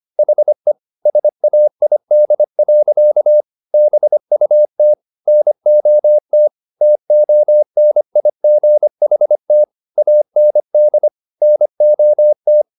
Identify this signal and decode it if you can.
Morse code, heard as a keyed tone, about 25 wpm, 600 Hz. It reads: HE SAID. BUT NOT TONIGHT AND NOT